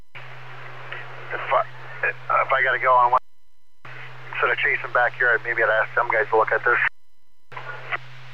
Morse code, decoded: none

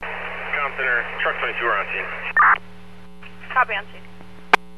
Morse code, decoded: none

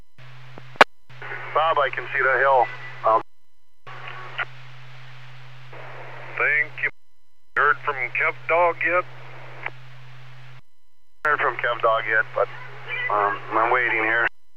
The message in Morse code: none

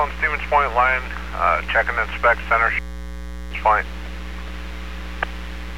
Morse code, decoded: none

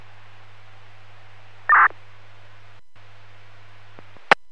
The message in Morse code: none

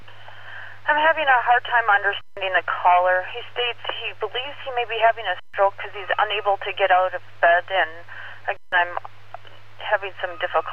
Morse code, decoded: none

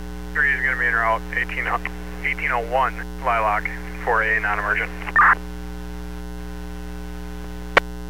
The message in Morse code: none

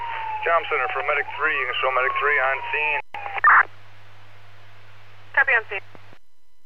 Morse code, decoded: none

